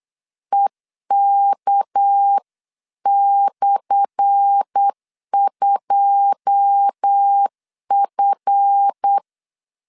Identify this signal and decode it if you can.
Morse code, heard as a keyed tone, about 8 wpm, 800 Hz. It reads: EK/2F
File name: LEZIONI / 10 SIGNS / 10A SLASH 8wpm 3452